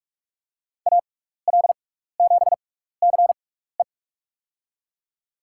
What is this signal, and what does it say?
Morse code, heard as a keyed tone, about 44 wpm, spaced as for 15 wpm, 700 Hz. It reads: AL7CE
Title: USA Call Signs - 6x Character Spacing - 45wpm